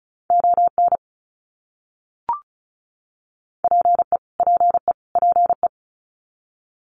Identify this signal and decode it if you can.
Morse code, heard as a keyed tone, about 34 wpm, 700 Hz. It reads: ON PE PE PE
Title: Canadian Provinces and Territories - Repeated 3x - 35wpm